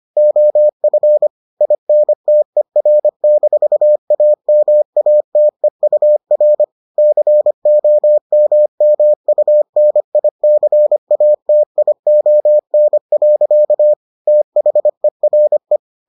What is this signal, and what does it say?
Morse code, heard as a keyed tone, about 25 wpm, 600 Hz. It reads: OF INTER-AMATEUR COMMUNICATION. THERE